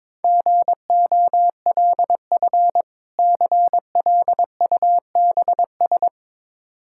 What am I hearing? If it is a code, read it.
Morse code, heard as a keyed tone, about 22 wpm, 690 Hz. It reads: GOLF CLUBS
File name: Difficult Words with Hints - 22wpm